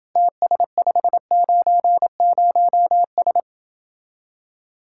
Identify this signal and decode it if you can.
Morse code, heard as a keyed tone, about 27 wpm, 700 Hz. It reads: TS590S